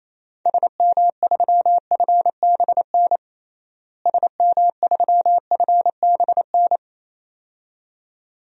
Morse code, heard as a keyed tone, about 28 wpm, 700 Hz. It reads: SM3FBN SM3FBN
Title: DXCC-100 Call Signs - 6 Characters - Repeated 2x - 28wpm